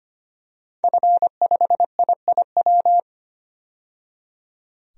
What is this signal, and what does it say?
Morse code, heard as a keyed tone, about 25 wpm, 700 Hz. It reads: F5IIW
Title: DXCC-100 Call Signs - 5 Characters - 25wpm